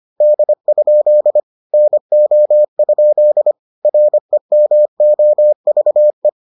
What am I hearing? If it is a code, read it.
Morse code, heard as a keyed tone, about 25 wpm, 600 Hz. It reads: D? NO? REMOVE